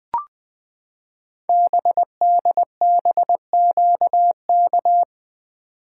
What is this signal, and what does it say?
Morse code, heard as a keyed tone, about 20 wpm, 700 Hz. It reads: BDBQK
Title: Code Groups - 5 Letters - 20wpm